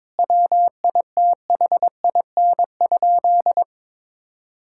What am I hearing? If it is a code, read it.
Morse code, heard as a keyed tone, about 22 wpm, 700 Hz. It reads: WITHIN?
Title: Sentences from Top 1000 Words - Review Part 6 - 22wpm